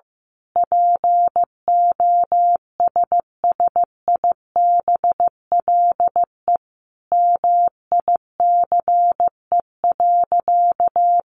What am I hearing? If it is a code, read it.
Morse code, heard as a keyed tone, about 15 wpm, 700 Hz. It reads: POSSIBLE MICE.
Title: A Sound Of Thunder 15wpm